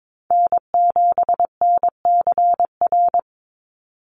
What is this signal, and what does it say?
Morse code, heard as a keyed tone, about 22 wpm, 700 Hz. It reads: N7NCR
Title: USA Call Signs 22wpm